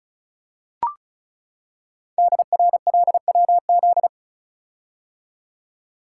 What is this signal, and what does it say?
Morse code, heard as a keyed tone, about 36 wpm, 690 Hz. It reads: DRLWZ
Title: Code Groups - 5 Letters - 35wpm